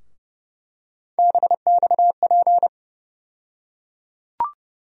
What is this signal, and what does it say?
Morse code, heard as a keyed tone, about 30 wpm, 700 Hz. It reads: BXP
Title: USA Call Sign Postfixes - 3 Letters 30wpm